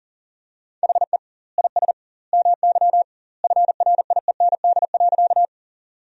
Morse code, heard as a keyed tone, about 40 wpm, 700 Hz. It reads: HE IS MY FRIEND.
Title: 2x Word-Spacing - Sentences from Top 300 Words - 40wpm